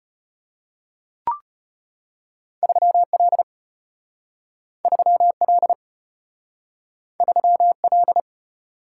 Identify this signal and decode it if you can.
Morse code, heard as a keyed tone, about 32 wpm, 700 Hz. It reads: 3L 3L 3L